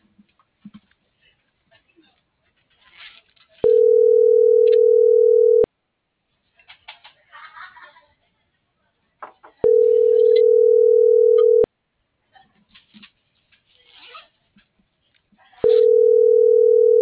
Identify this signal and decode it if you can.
Ringback tone (North America), 440 + 480 Hz.
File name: CENTRAL BLN / out marco.llanos